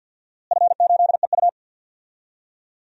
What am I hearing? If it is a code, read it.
Morse code, heard as a keyed tone, about 51 wpm, 700 Hz. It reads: F8EU
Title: DXCC-100 Call Signs - 4 Characters - Repeated 2x - 50wpm